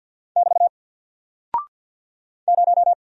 Morse code, heard as a keyed tone, about 48 wpm, 700 Hz. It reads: <BT> 0
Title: Single Character (Letter:Number:Punctuation:Prosign) Alt 50wpm